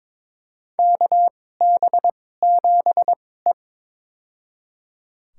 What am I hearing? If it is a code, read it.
Morse code, heard as a keyed tone, about 22 wpm, spaced as for 16 wpm, 700 Hz. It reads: KB7E
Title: USA Call Signs - 2x Character Spacing - 22wpm